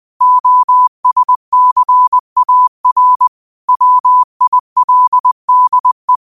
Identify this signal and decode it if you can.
Morse code, heard as a keyed tone, about 20 wpm, 1000 Hz. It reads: OSCAR WILDE
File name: qotdfiles / 20wpm